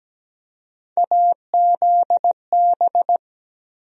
Morse code, heard as a keyed tone, about 17 wpm, 700 Hz. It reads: AZB
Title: USA Call Sign Postfixes - 3 Letters 17wpm